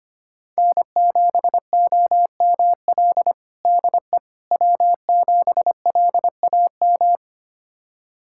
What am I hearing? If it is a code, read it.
Morse code, heard as a keyed tone, about 25 wpm, 700 Hz. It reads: N7OML DE W7LAM
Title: USA Callsign DE Callsign - 5 Characters - 25wpm